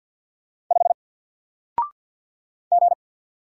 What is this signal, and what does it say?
Morse code, heard as a keyed tone, about 48 wpm, 700 Hz. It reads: 5 G
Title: Single Character (Letter:Number:Punctuation:Prosign) Alt 50wpm